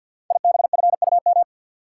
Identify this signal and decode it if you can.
Morse code, heard as a keyed tone, about 52 wpm, 700 Hz. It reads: IBFUM
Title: Code Groups - 5 Letter:Numbers - 50wpm